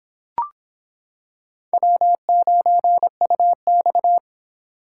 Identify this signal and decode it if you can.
Morse code, heard as a keyed tone, about 26 wpm, 700 Hz. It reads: W9UX